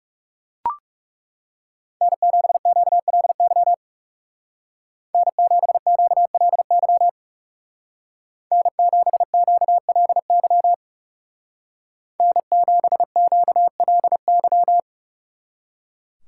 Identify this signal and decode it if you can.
Morse code, heard as a keyed tone, about 36 wpm, 700 Hz. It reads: N7QLY N7QLY N7QLY N7QLY
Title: Speed Racing - USA 1x3 Call Signs 30wpm